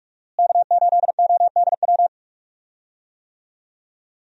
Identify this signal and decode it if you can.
Morse code, heard as a keyed tone, about 46 wpm, 700 Hz. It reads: K8ODW
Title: USA Call Signs - 5 Characters - Repeated 3x - 45wpm